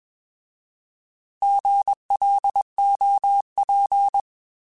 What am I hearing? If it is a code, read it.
Morse code, heard as a keyed tone, about 20 wpm, 800 Hz. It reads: GLOP
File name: Alphabetical / Alphabetical mp3s / Gs @20